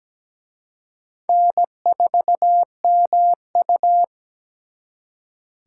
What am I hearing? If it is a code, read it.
Morse code, heard as a keyed tone, about 20 wpm, 700 Hz. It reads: N4MU
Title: Speed Racing - USA 1x2 Call Signs 17wpm